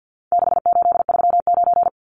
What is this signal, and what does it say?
Morse code, heard as a keyed tone, about 50 wpm, 700 Hz. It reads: 6839
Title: Numbers - 4-Digit 50wpm